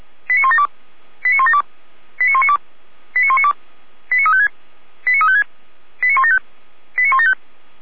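Five-tone selective call (ZVEI), heard as quick runs of five tones, tones of about 70 ms. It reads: 89172 89172 89192 89192 89356 89356 89175 89175
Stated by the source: ZVEI five-tone selective call 89582, received by SDR at feuerwehr-orlamuende.org (fire brigade Orlamünde).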